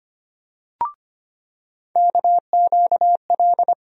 Morse code, heard as a keyed tone, about 25 wpm, 700 Hz. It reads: KQL
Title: USA Call Sign Postfixes - 3 Letters 25wpm